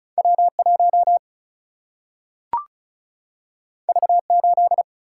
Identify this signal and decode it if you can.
Morse code, heard as a keyed tone, about 36 wpm, 700 Hz. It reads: W 1 V 8
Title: Single-Letter Single-Number 35wpm